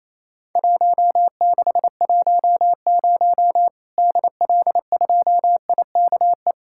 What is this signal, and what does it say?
Morse code, heard as a keyed tone, about 28 wpm, 700 Hz. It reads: 1610 DL2IKE
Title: Worked All Europe DX Contest - QTC Exchange - 28wpm